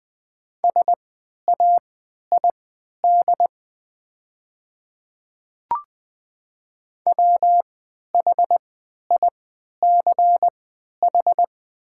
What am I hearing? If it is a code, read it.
Morse code, heard as a keyed tone, about 20 wpm, spaced as for 11 wpm, 700 Hz. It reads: SAID WHICH
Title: 3x Character Spacing - Top 100 Words - 20wpm